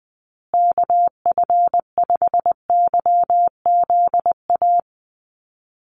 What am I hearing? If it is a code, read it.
Morse code, heard as a keyed tone, about 20 wpm, 700 Hz. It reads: KF5YZA
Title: USA Call Signs Repeated 3x 20wpm